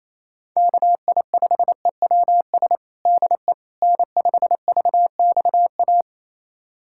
Morse code, heard as a keyed tone, about 28 wpm, 700 Hz. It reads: KI5EWS DE N5VXA
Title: USA Callsign DE Callsign 28wpm